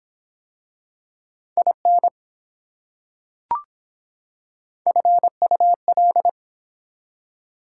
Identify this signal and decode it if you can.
Morse code, heard as a keyed tone, about 24 wpm, 700 Hz. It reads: IN FUL